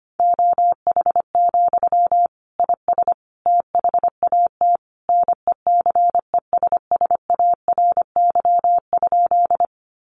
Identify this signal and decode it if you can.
Morse code, heard as a keyed tone, about 25 wpm, 700 Hz. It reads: OH, IS THAT NECESSARY?
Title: Sentences from Top 800 Words 25wpm